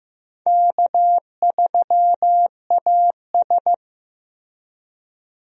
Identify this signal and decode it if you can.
Morse code, heard as a keyed tone, about 16 wpm, 700 Hz. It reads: K3AS
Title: USA Call Signs - 4 Characters - Repeated 2x - 15wpm